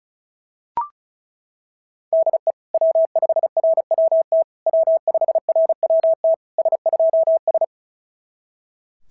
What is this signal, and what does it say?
Morse code, heard as a keyed tone, about 37 wpm, 650 Hz. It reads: DE W5RWT W5RWT S2S
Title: SOTA QSO Practice 35wpm